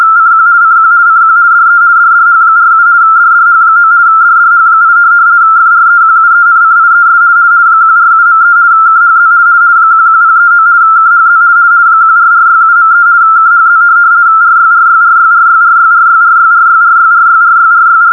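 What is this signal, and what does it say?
RTTY, about 50 baud. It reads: CQ CQ CQ DE DDH47 DDH9 DDH8 FREQUENCIES 147.3 KHZ 11039 KHZ 14467.3 KHZ RYRYRYRYRYRYRYRYRYRYRYRYR
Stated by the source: RTTY test tape from Hamburg Meteo (DWD, Deutscher Wetter Dienst) telegraphic weather service on 147.3, recorded from longwave with only 85 Hz shift.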